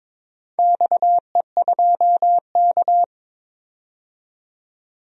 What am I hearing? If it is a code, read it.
Morse code, heard as a keyed tone, about 22 wpm, 700 Hz. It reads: XE2K
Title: DXCC-100 Call Signs - 4 Characters - 22wpm